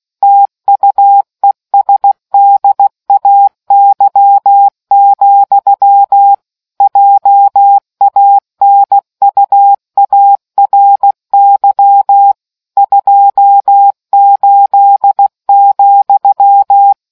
Morse code, heard as a keyed tone, about 16 wpm, 800 Hz. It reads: TUESDAY, JANUARY 28,